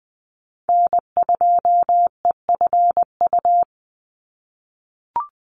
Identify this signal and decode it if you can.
Morse code, heard as a keyed tone, about 20 wpm, 700 Hz. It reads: N2EFU
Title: USA Call Signs 20wpm